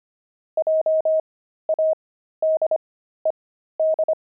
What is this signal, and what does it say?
Morse code, heard as a keyed tone, about 25 wpm, spaced as for 13 wpm, 630 Hz. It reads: JADED
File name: pangram-fs-13-cs-25-jaded